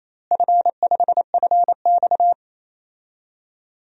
Morse code, heard as a keyed tone, about 28 wpm, 700 Hz. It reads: F5FX
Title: DXCC-100 Call Signs - 4 Characters - Repeated 3x - 28wpm